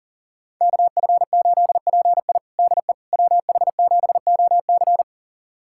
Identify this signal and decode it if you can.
Morse code, heard as a keyed tone, about 37 wpm, 700 Hz. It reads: KF8PI DE WH7OC
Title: USA Callsign DE Callsign - 5 Characters - 40wpm